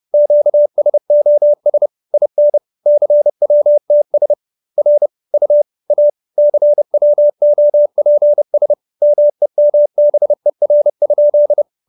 Morse code, heard as a keyed tone, about 30 wpm, 590 Hz. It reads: QSOS IN CWTS R U A CWOPS MEMBER?